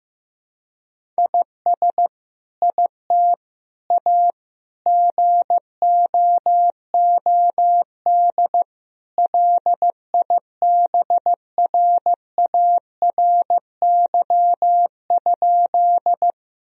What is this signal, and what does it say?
Morse code, heard as a keyed tone, about 15 wpm, 700 Hz. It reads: IS IT A GOOD LIBRARY?
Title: Sentences from Top 1000 Words - Review Part 4 - 15wpm